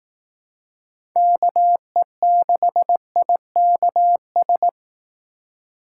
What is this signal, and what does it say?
Morse code, heard as a keyed tone, about 18 wpm, 700 Hz. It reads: KE6IKS
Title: Speed Racing - USA 2x3 Call Signs 15wpm